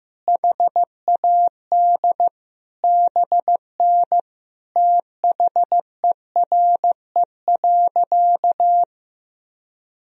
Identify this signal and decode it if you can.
Morse code, heard as a keyed tone, about 15 wpm, 700 Hz. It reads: HAD BN THERE.
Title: Sentences from Top 300 Words and Abbreviations 15wpm